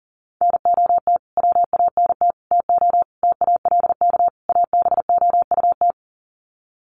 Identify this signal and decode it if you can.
Morse code, heard as a keyed tone, about 40 wpm, 700 Hz. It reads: NOT WANT TO TALK ABOUT
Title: Sets of 5 Words 40wpm